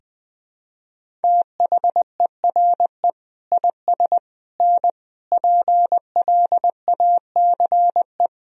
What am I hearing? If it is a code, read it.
Morse code, heard as a keyed tone, about 21 wpm, 700 Hz. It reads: THERE IS N PLACE
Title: Sentences from Top 200 Words and Abbreviations 20wpm